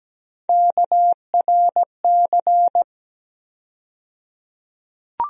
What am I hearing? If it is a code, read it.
Morse code, heard as a keyed tone, about 17 wpm, 700 Hz. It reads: KRC
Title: USA Call Sign Postfixes - 3 Letters 17wpm